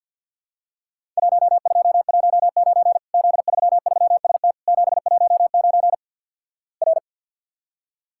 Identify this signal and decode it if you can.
Morse code, heard as a keyed tone, about 48 wpm, 700 Hz. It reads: 1219 Z33ST 719 R
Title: Worked All Europe DX Contest - QTC Exchange - 50wpm